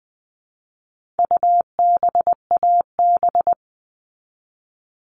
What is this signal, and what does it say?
Morse code, heard as a keyed tone, about 20 wpm, 700 Hz. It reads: UBAB
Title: All Cut Numbers - 4-Digit 20wpm